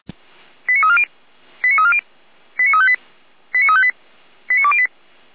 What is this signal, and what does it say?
Five-tone selective call (ZVEI), heard as quick runs of five tones, tones of about 70 ms. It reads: 89340 89340 89358 89358 89208
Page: SDR/ZVEI